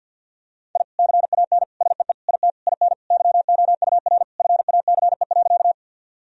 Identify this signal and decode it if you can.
Morse code, heard as a keyed tone, about 50 wpm, 700 Hz. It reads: I CAN SEE IT IN YOUR FACE.
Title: Sentences from Top 1000 Words - Review Part 2 - 50wpm